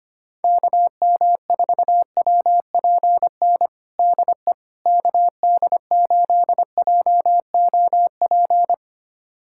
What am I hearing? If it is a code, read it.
Morse code, heard as a keyed tone, about 25 wpm, 700 Hz. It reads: KM4WPN DE KD8JOP
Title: USA Callsign DE Callsign - 6 Characters - 25wpm